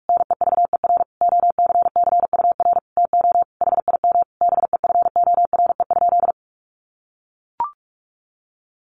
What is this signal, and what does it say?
Morse code, heard as a keyed tone, about 44 wpm, 700 Hz. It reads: NEVER OCCUR TO HIM BEFORE?
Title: Sentences from Top 900 Words 45wpm